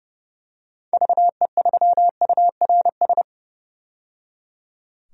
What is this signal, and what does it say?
Morse code, heard as a keyed tone, about 30 wpm, 700 Hz. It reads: VE3URS
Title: DXCC-100 Call Signs - 6 Characters - 30wpm